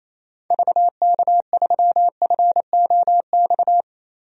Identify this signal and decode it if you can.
Morse code, heard as a keyed tone, about 28 wpm, 700 Hz. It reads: VK3FOX
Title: DXCC-100 Call Signs - 6 Characters - Repeated 3x - 28wpm